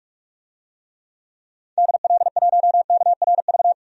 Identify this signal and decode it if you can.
Morse code, heard as a keyed tone, about 43 wpm, 700 Hz. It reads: DL1KRV
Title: DXCC-100 Call Signs - 6 Characters - 45wpm